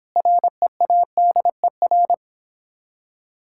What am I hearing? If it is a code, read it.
Morse code, heard as a keyed tone, about 26 wpm, 700 Hz. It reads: READER